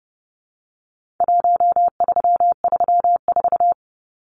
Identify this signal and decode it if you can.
Morse code, heard as a keyed tone, about 28 wpm, 700 Hz. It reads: 1334
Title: Numbers - Contest Style - 4-Digit 30wpm